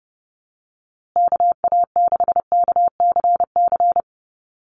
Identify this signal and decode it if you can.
Morse code, heard as a keyed tone, about 30 wpm, 700 Hz. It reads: KA6KCC
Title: USA Call Signs Repeated 2x 30wpm